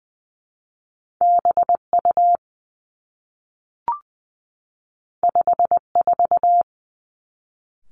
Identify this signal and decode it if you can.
Morse code, heard as a keyed tone, about 20 wpm, 700 Hz. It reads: B U 5 4
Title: Two Challenging Letter-Number Combinations 20wpm